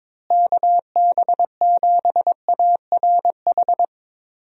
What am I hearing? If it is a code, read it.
Morse code, heard as a keyed tone, about 22 wpm, 700 Hz. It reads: KB7ARH
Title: USA Call Signs - 6 Characters - Repeated 3x - 22wpm